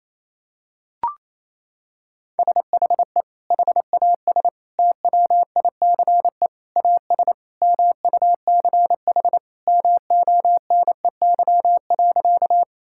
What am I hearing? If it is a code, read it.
Morse code, heard as a keyed tone, about 27 wpm, 700 Hz. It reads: SHE HAS TWICE AS MUCH MONEY.
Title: Sentences from Top 1600 Words 28wpm